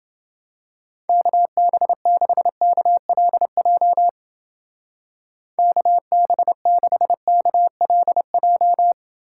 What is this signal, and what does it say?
Morse code, heard as a keyed tone, about 28 wpm, 700 Hz. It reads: KB6KLJ KB6KLJ